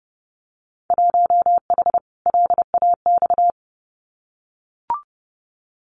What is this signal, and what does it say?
Morse code, heard as a keyed tone, about 30 wpm, 700 Hz. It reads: 1H LAX
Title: Winter Field Day Exchanges 30wpm